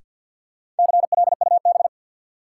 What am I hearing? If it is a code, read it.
Morse code, heard as a keyed tone, about 52 wpm, 700 Hz. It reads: CLUB
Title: Sentences from Top 1200 Words - New Word Drill - 50wpm